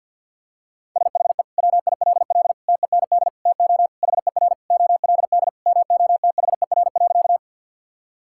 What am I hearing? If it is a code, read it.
Morse code, heard as a keyed tone, about 46 wpm, 700 Hz. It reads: SHE WILL TEND TO HER OLD MOTHER.
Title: Sentences from Top 1000 Words - Review Part 5 - 50wpm